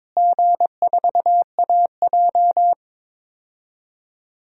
Morse code, heard as a keyed tone, about 22 wpm, 700 Hz. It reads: G4AJ
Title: DXCC-100 Call Signs - 4 Characters - Repeated 2x - 22wpm